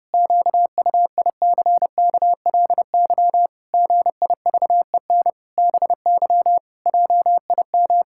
Morse code, heard as a keyed tone, about 30 wpm, 700 Hz. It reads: QUICKLY GIVEN BY JIM
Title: Pangram Challenge v2 - 30wpm